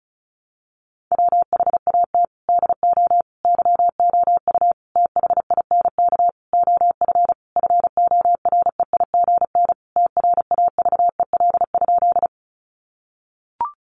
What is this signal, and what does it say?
Morse code, heard as a keyed tone, about 34 wpm, 700 Hz. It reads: WHAT DO YOU THINK OF FOREIGN TRAVEL?
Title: Sentences from Top 1400 Words 35wpm